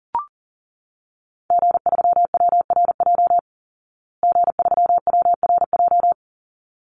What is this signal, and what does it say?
Morse code, heard as a keyed tone, about 40 wpm, 700 Hz. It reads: G3WRJ G3WRJ
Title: Call Signs Repeated 3x - 40wpm